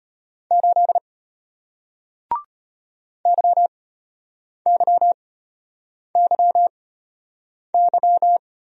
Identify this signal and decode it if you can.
Morse code, heard as a keyed tone, about 33 wpm, spaced as for 8 wpm, 700 Hz. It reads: 8 YYYY